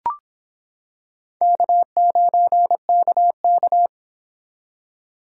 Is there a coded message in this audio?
Yes: Morse code from 1.4 to 3.9 s, heard as a keyed tone, about 25 wpm, 700 Hz. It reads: K9KK